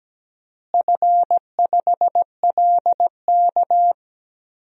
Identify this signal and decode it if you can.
Morse code, heard as a keyed tone, about 17 wpm, 700 Hz. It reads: F5LK
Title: DXCC-100 Call Signs - 4 Characters - Repeated 3x - 17wpm